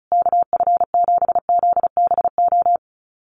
Morse code, heard as a keyed tone, about 34 wpm, 700 Hz. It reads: KF7ZBO
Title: USA Call Signs Repeated 3x 35wpm